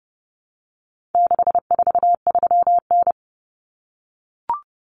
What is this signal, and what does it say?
Morse code, heard as a keyed tone, about 30 wpm, 700 Hz. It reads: 643N
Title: Common Cut Numbers - 4-Digit 30wpm